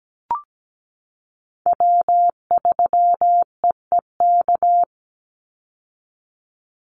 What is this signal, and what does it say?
Morse code, heard as a keyed tone, about 17 wpm, 700 Hz. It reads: W3EEK
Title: Call Signs Repeated 3x - 17wpm